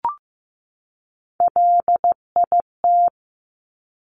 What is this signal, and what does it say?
Morse code, heard as a keyed tone, about 15 wpm, 700 Hz. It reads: LIT
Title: Top Three Letter Combinations in Words 15wpm